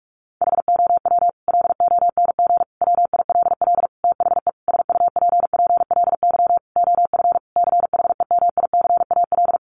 Morse code, heard as a keyed tone, about 44 wpm, 700 Hz. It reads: HOW LONG WILL THE SUPPLY OF CHEMICAL
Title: Sentences from Top 1200 Words 45wpm